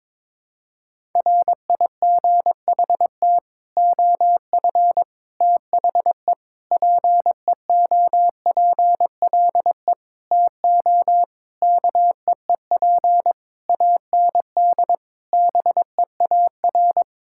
Morse code, heard as a keyed tone, about 21 wpm, 700 Hz. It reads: RIGHT OF THE PEOPLE TO KEEP AND BEAR
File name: Sets of 9 Words - 22wpm